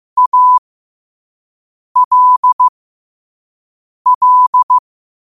Morse code, heard as a keyed tone, about 15 wpm, spaced as for 5 wpm, 1000 Hz. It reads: ALL